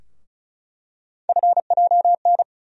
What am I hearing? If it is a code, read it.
Morse code, heard as a keyed tone, about 36 wpm, 700 Hz. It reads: FJN
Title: USA Call Sign Postfixes - 3 Letters 35wpm